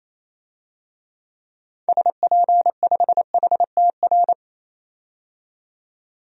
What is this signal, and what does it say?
Morse code, heard as a keyed tone, about 28 wpm, 700 Hz. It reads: SP5HTR
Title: DXCC-100 Call Signs - 6 Characters - 28wpm